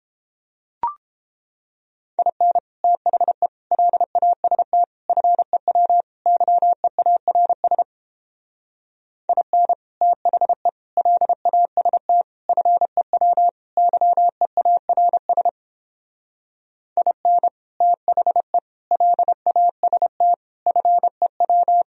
Morse code, heard as a keyed tone, about 29 wpm, 700 Hz. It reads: IN THE LAST FEW YEARS IN THE LAST FEW YEARS IN THE LAST FEW